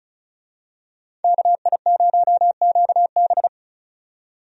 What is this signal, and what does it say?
Morse code, heard as a keyed tone, about 34 wpm, 700 Hz. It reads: KI0QB
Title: USA Call Signs - 5 Characters - 35wpm